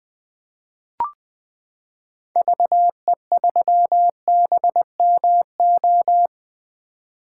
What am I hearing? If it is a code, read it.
Morse code, heard as a keyed tone, about 20 wpm, 700 Hz. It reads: VE3BMO
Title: 2023 POTA Activator Call Signs - 20wpm